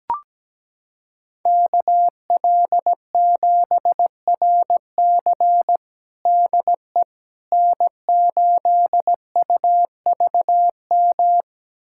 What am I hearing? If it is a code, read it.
Morse code, heard as a keyed tone, about 17 wpm, 700 Hz. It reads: KL7RC DE N8UVM
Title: USA Callsign DE Callsign - 5 Characters - 17wpm